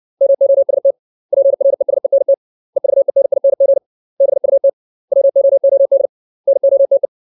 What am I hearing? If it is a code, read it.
Morse code, heard as a keyed tone, about 60 wpm, 550 Hz. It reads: MOST PRESENT EVENING BUT WOOD NOTE
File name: 632 words / 632 Words 60WPM